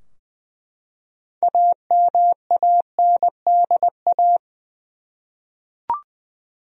Morse code, heard as a keyed tone, about 20 wpm, 700 Hz. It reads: AMANDA
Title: USA Common Names - 20wpm